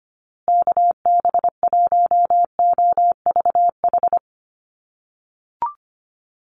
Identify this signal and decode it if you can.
Morse code, heard as a keyed tone, about 25 wpm, 700 Hz. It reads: KB1OVH
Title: USA Call Signs Repeated 2x 25wpm